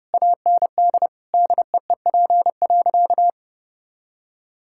Morse code, heard as a keyed tone, about 32 wpm, 700 Hz. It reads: AND DEEP.
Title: Sentences from Top 1600 Words 30wpm